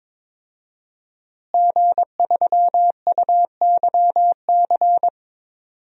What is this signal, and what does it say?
Morse code, heard as a keyed tone, about 22 wpm, 700 Hz. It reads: G3UYC
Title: DXCC-100 Call Signs - 5 Characters - Repeated 3x - 22wpm